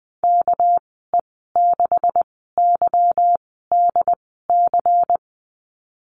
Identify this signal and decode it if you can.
Morse code, heard as a keyed tone, about 20 wpm, spaced as for 15 wpm, 700 Hz. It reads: KE6YDC
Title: 231 Lesson 37g - Letter x - Call Signs 20wpm